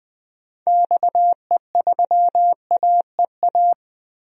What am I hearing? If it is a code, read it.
Morse code, heard as a keyed tone, about 21 wpm, 700 Hz. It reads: XE3AEA
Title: DXCC-100 Call Signs - 6 Characters - Repeated 3x - 20wpm